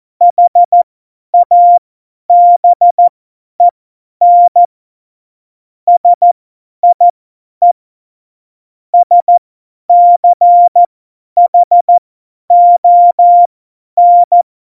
Morse code, heard as a keyed tone, about 14 wpm, spaced as for 10 wpm, 700 Hz. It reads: HABEN SIE SCHON
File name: Morsekurs / Witze / Witz 15. Januar 2019, 50 F70